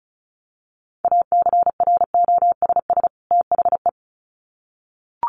Morse code, heard as a keyed tone, about 34 wpm, 700 Hz. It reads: ACROSS THE
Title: Sets of 2 Words 35wpm